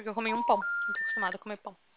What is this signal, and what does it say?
Special information tone, with speech.